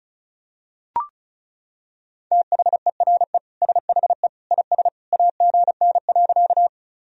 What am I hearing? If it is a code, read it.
Morse code, heard as a keyed tone, about 36 wpm, 690 Hz. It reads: THERE SHE IS AGN.
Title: Sentences from Top 200 Words and Abbreviations 35wpm